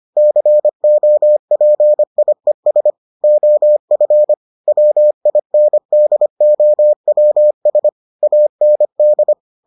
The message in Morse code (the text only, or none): COPIES OF WINDOWS AND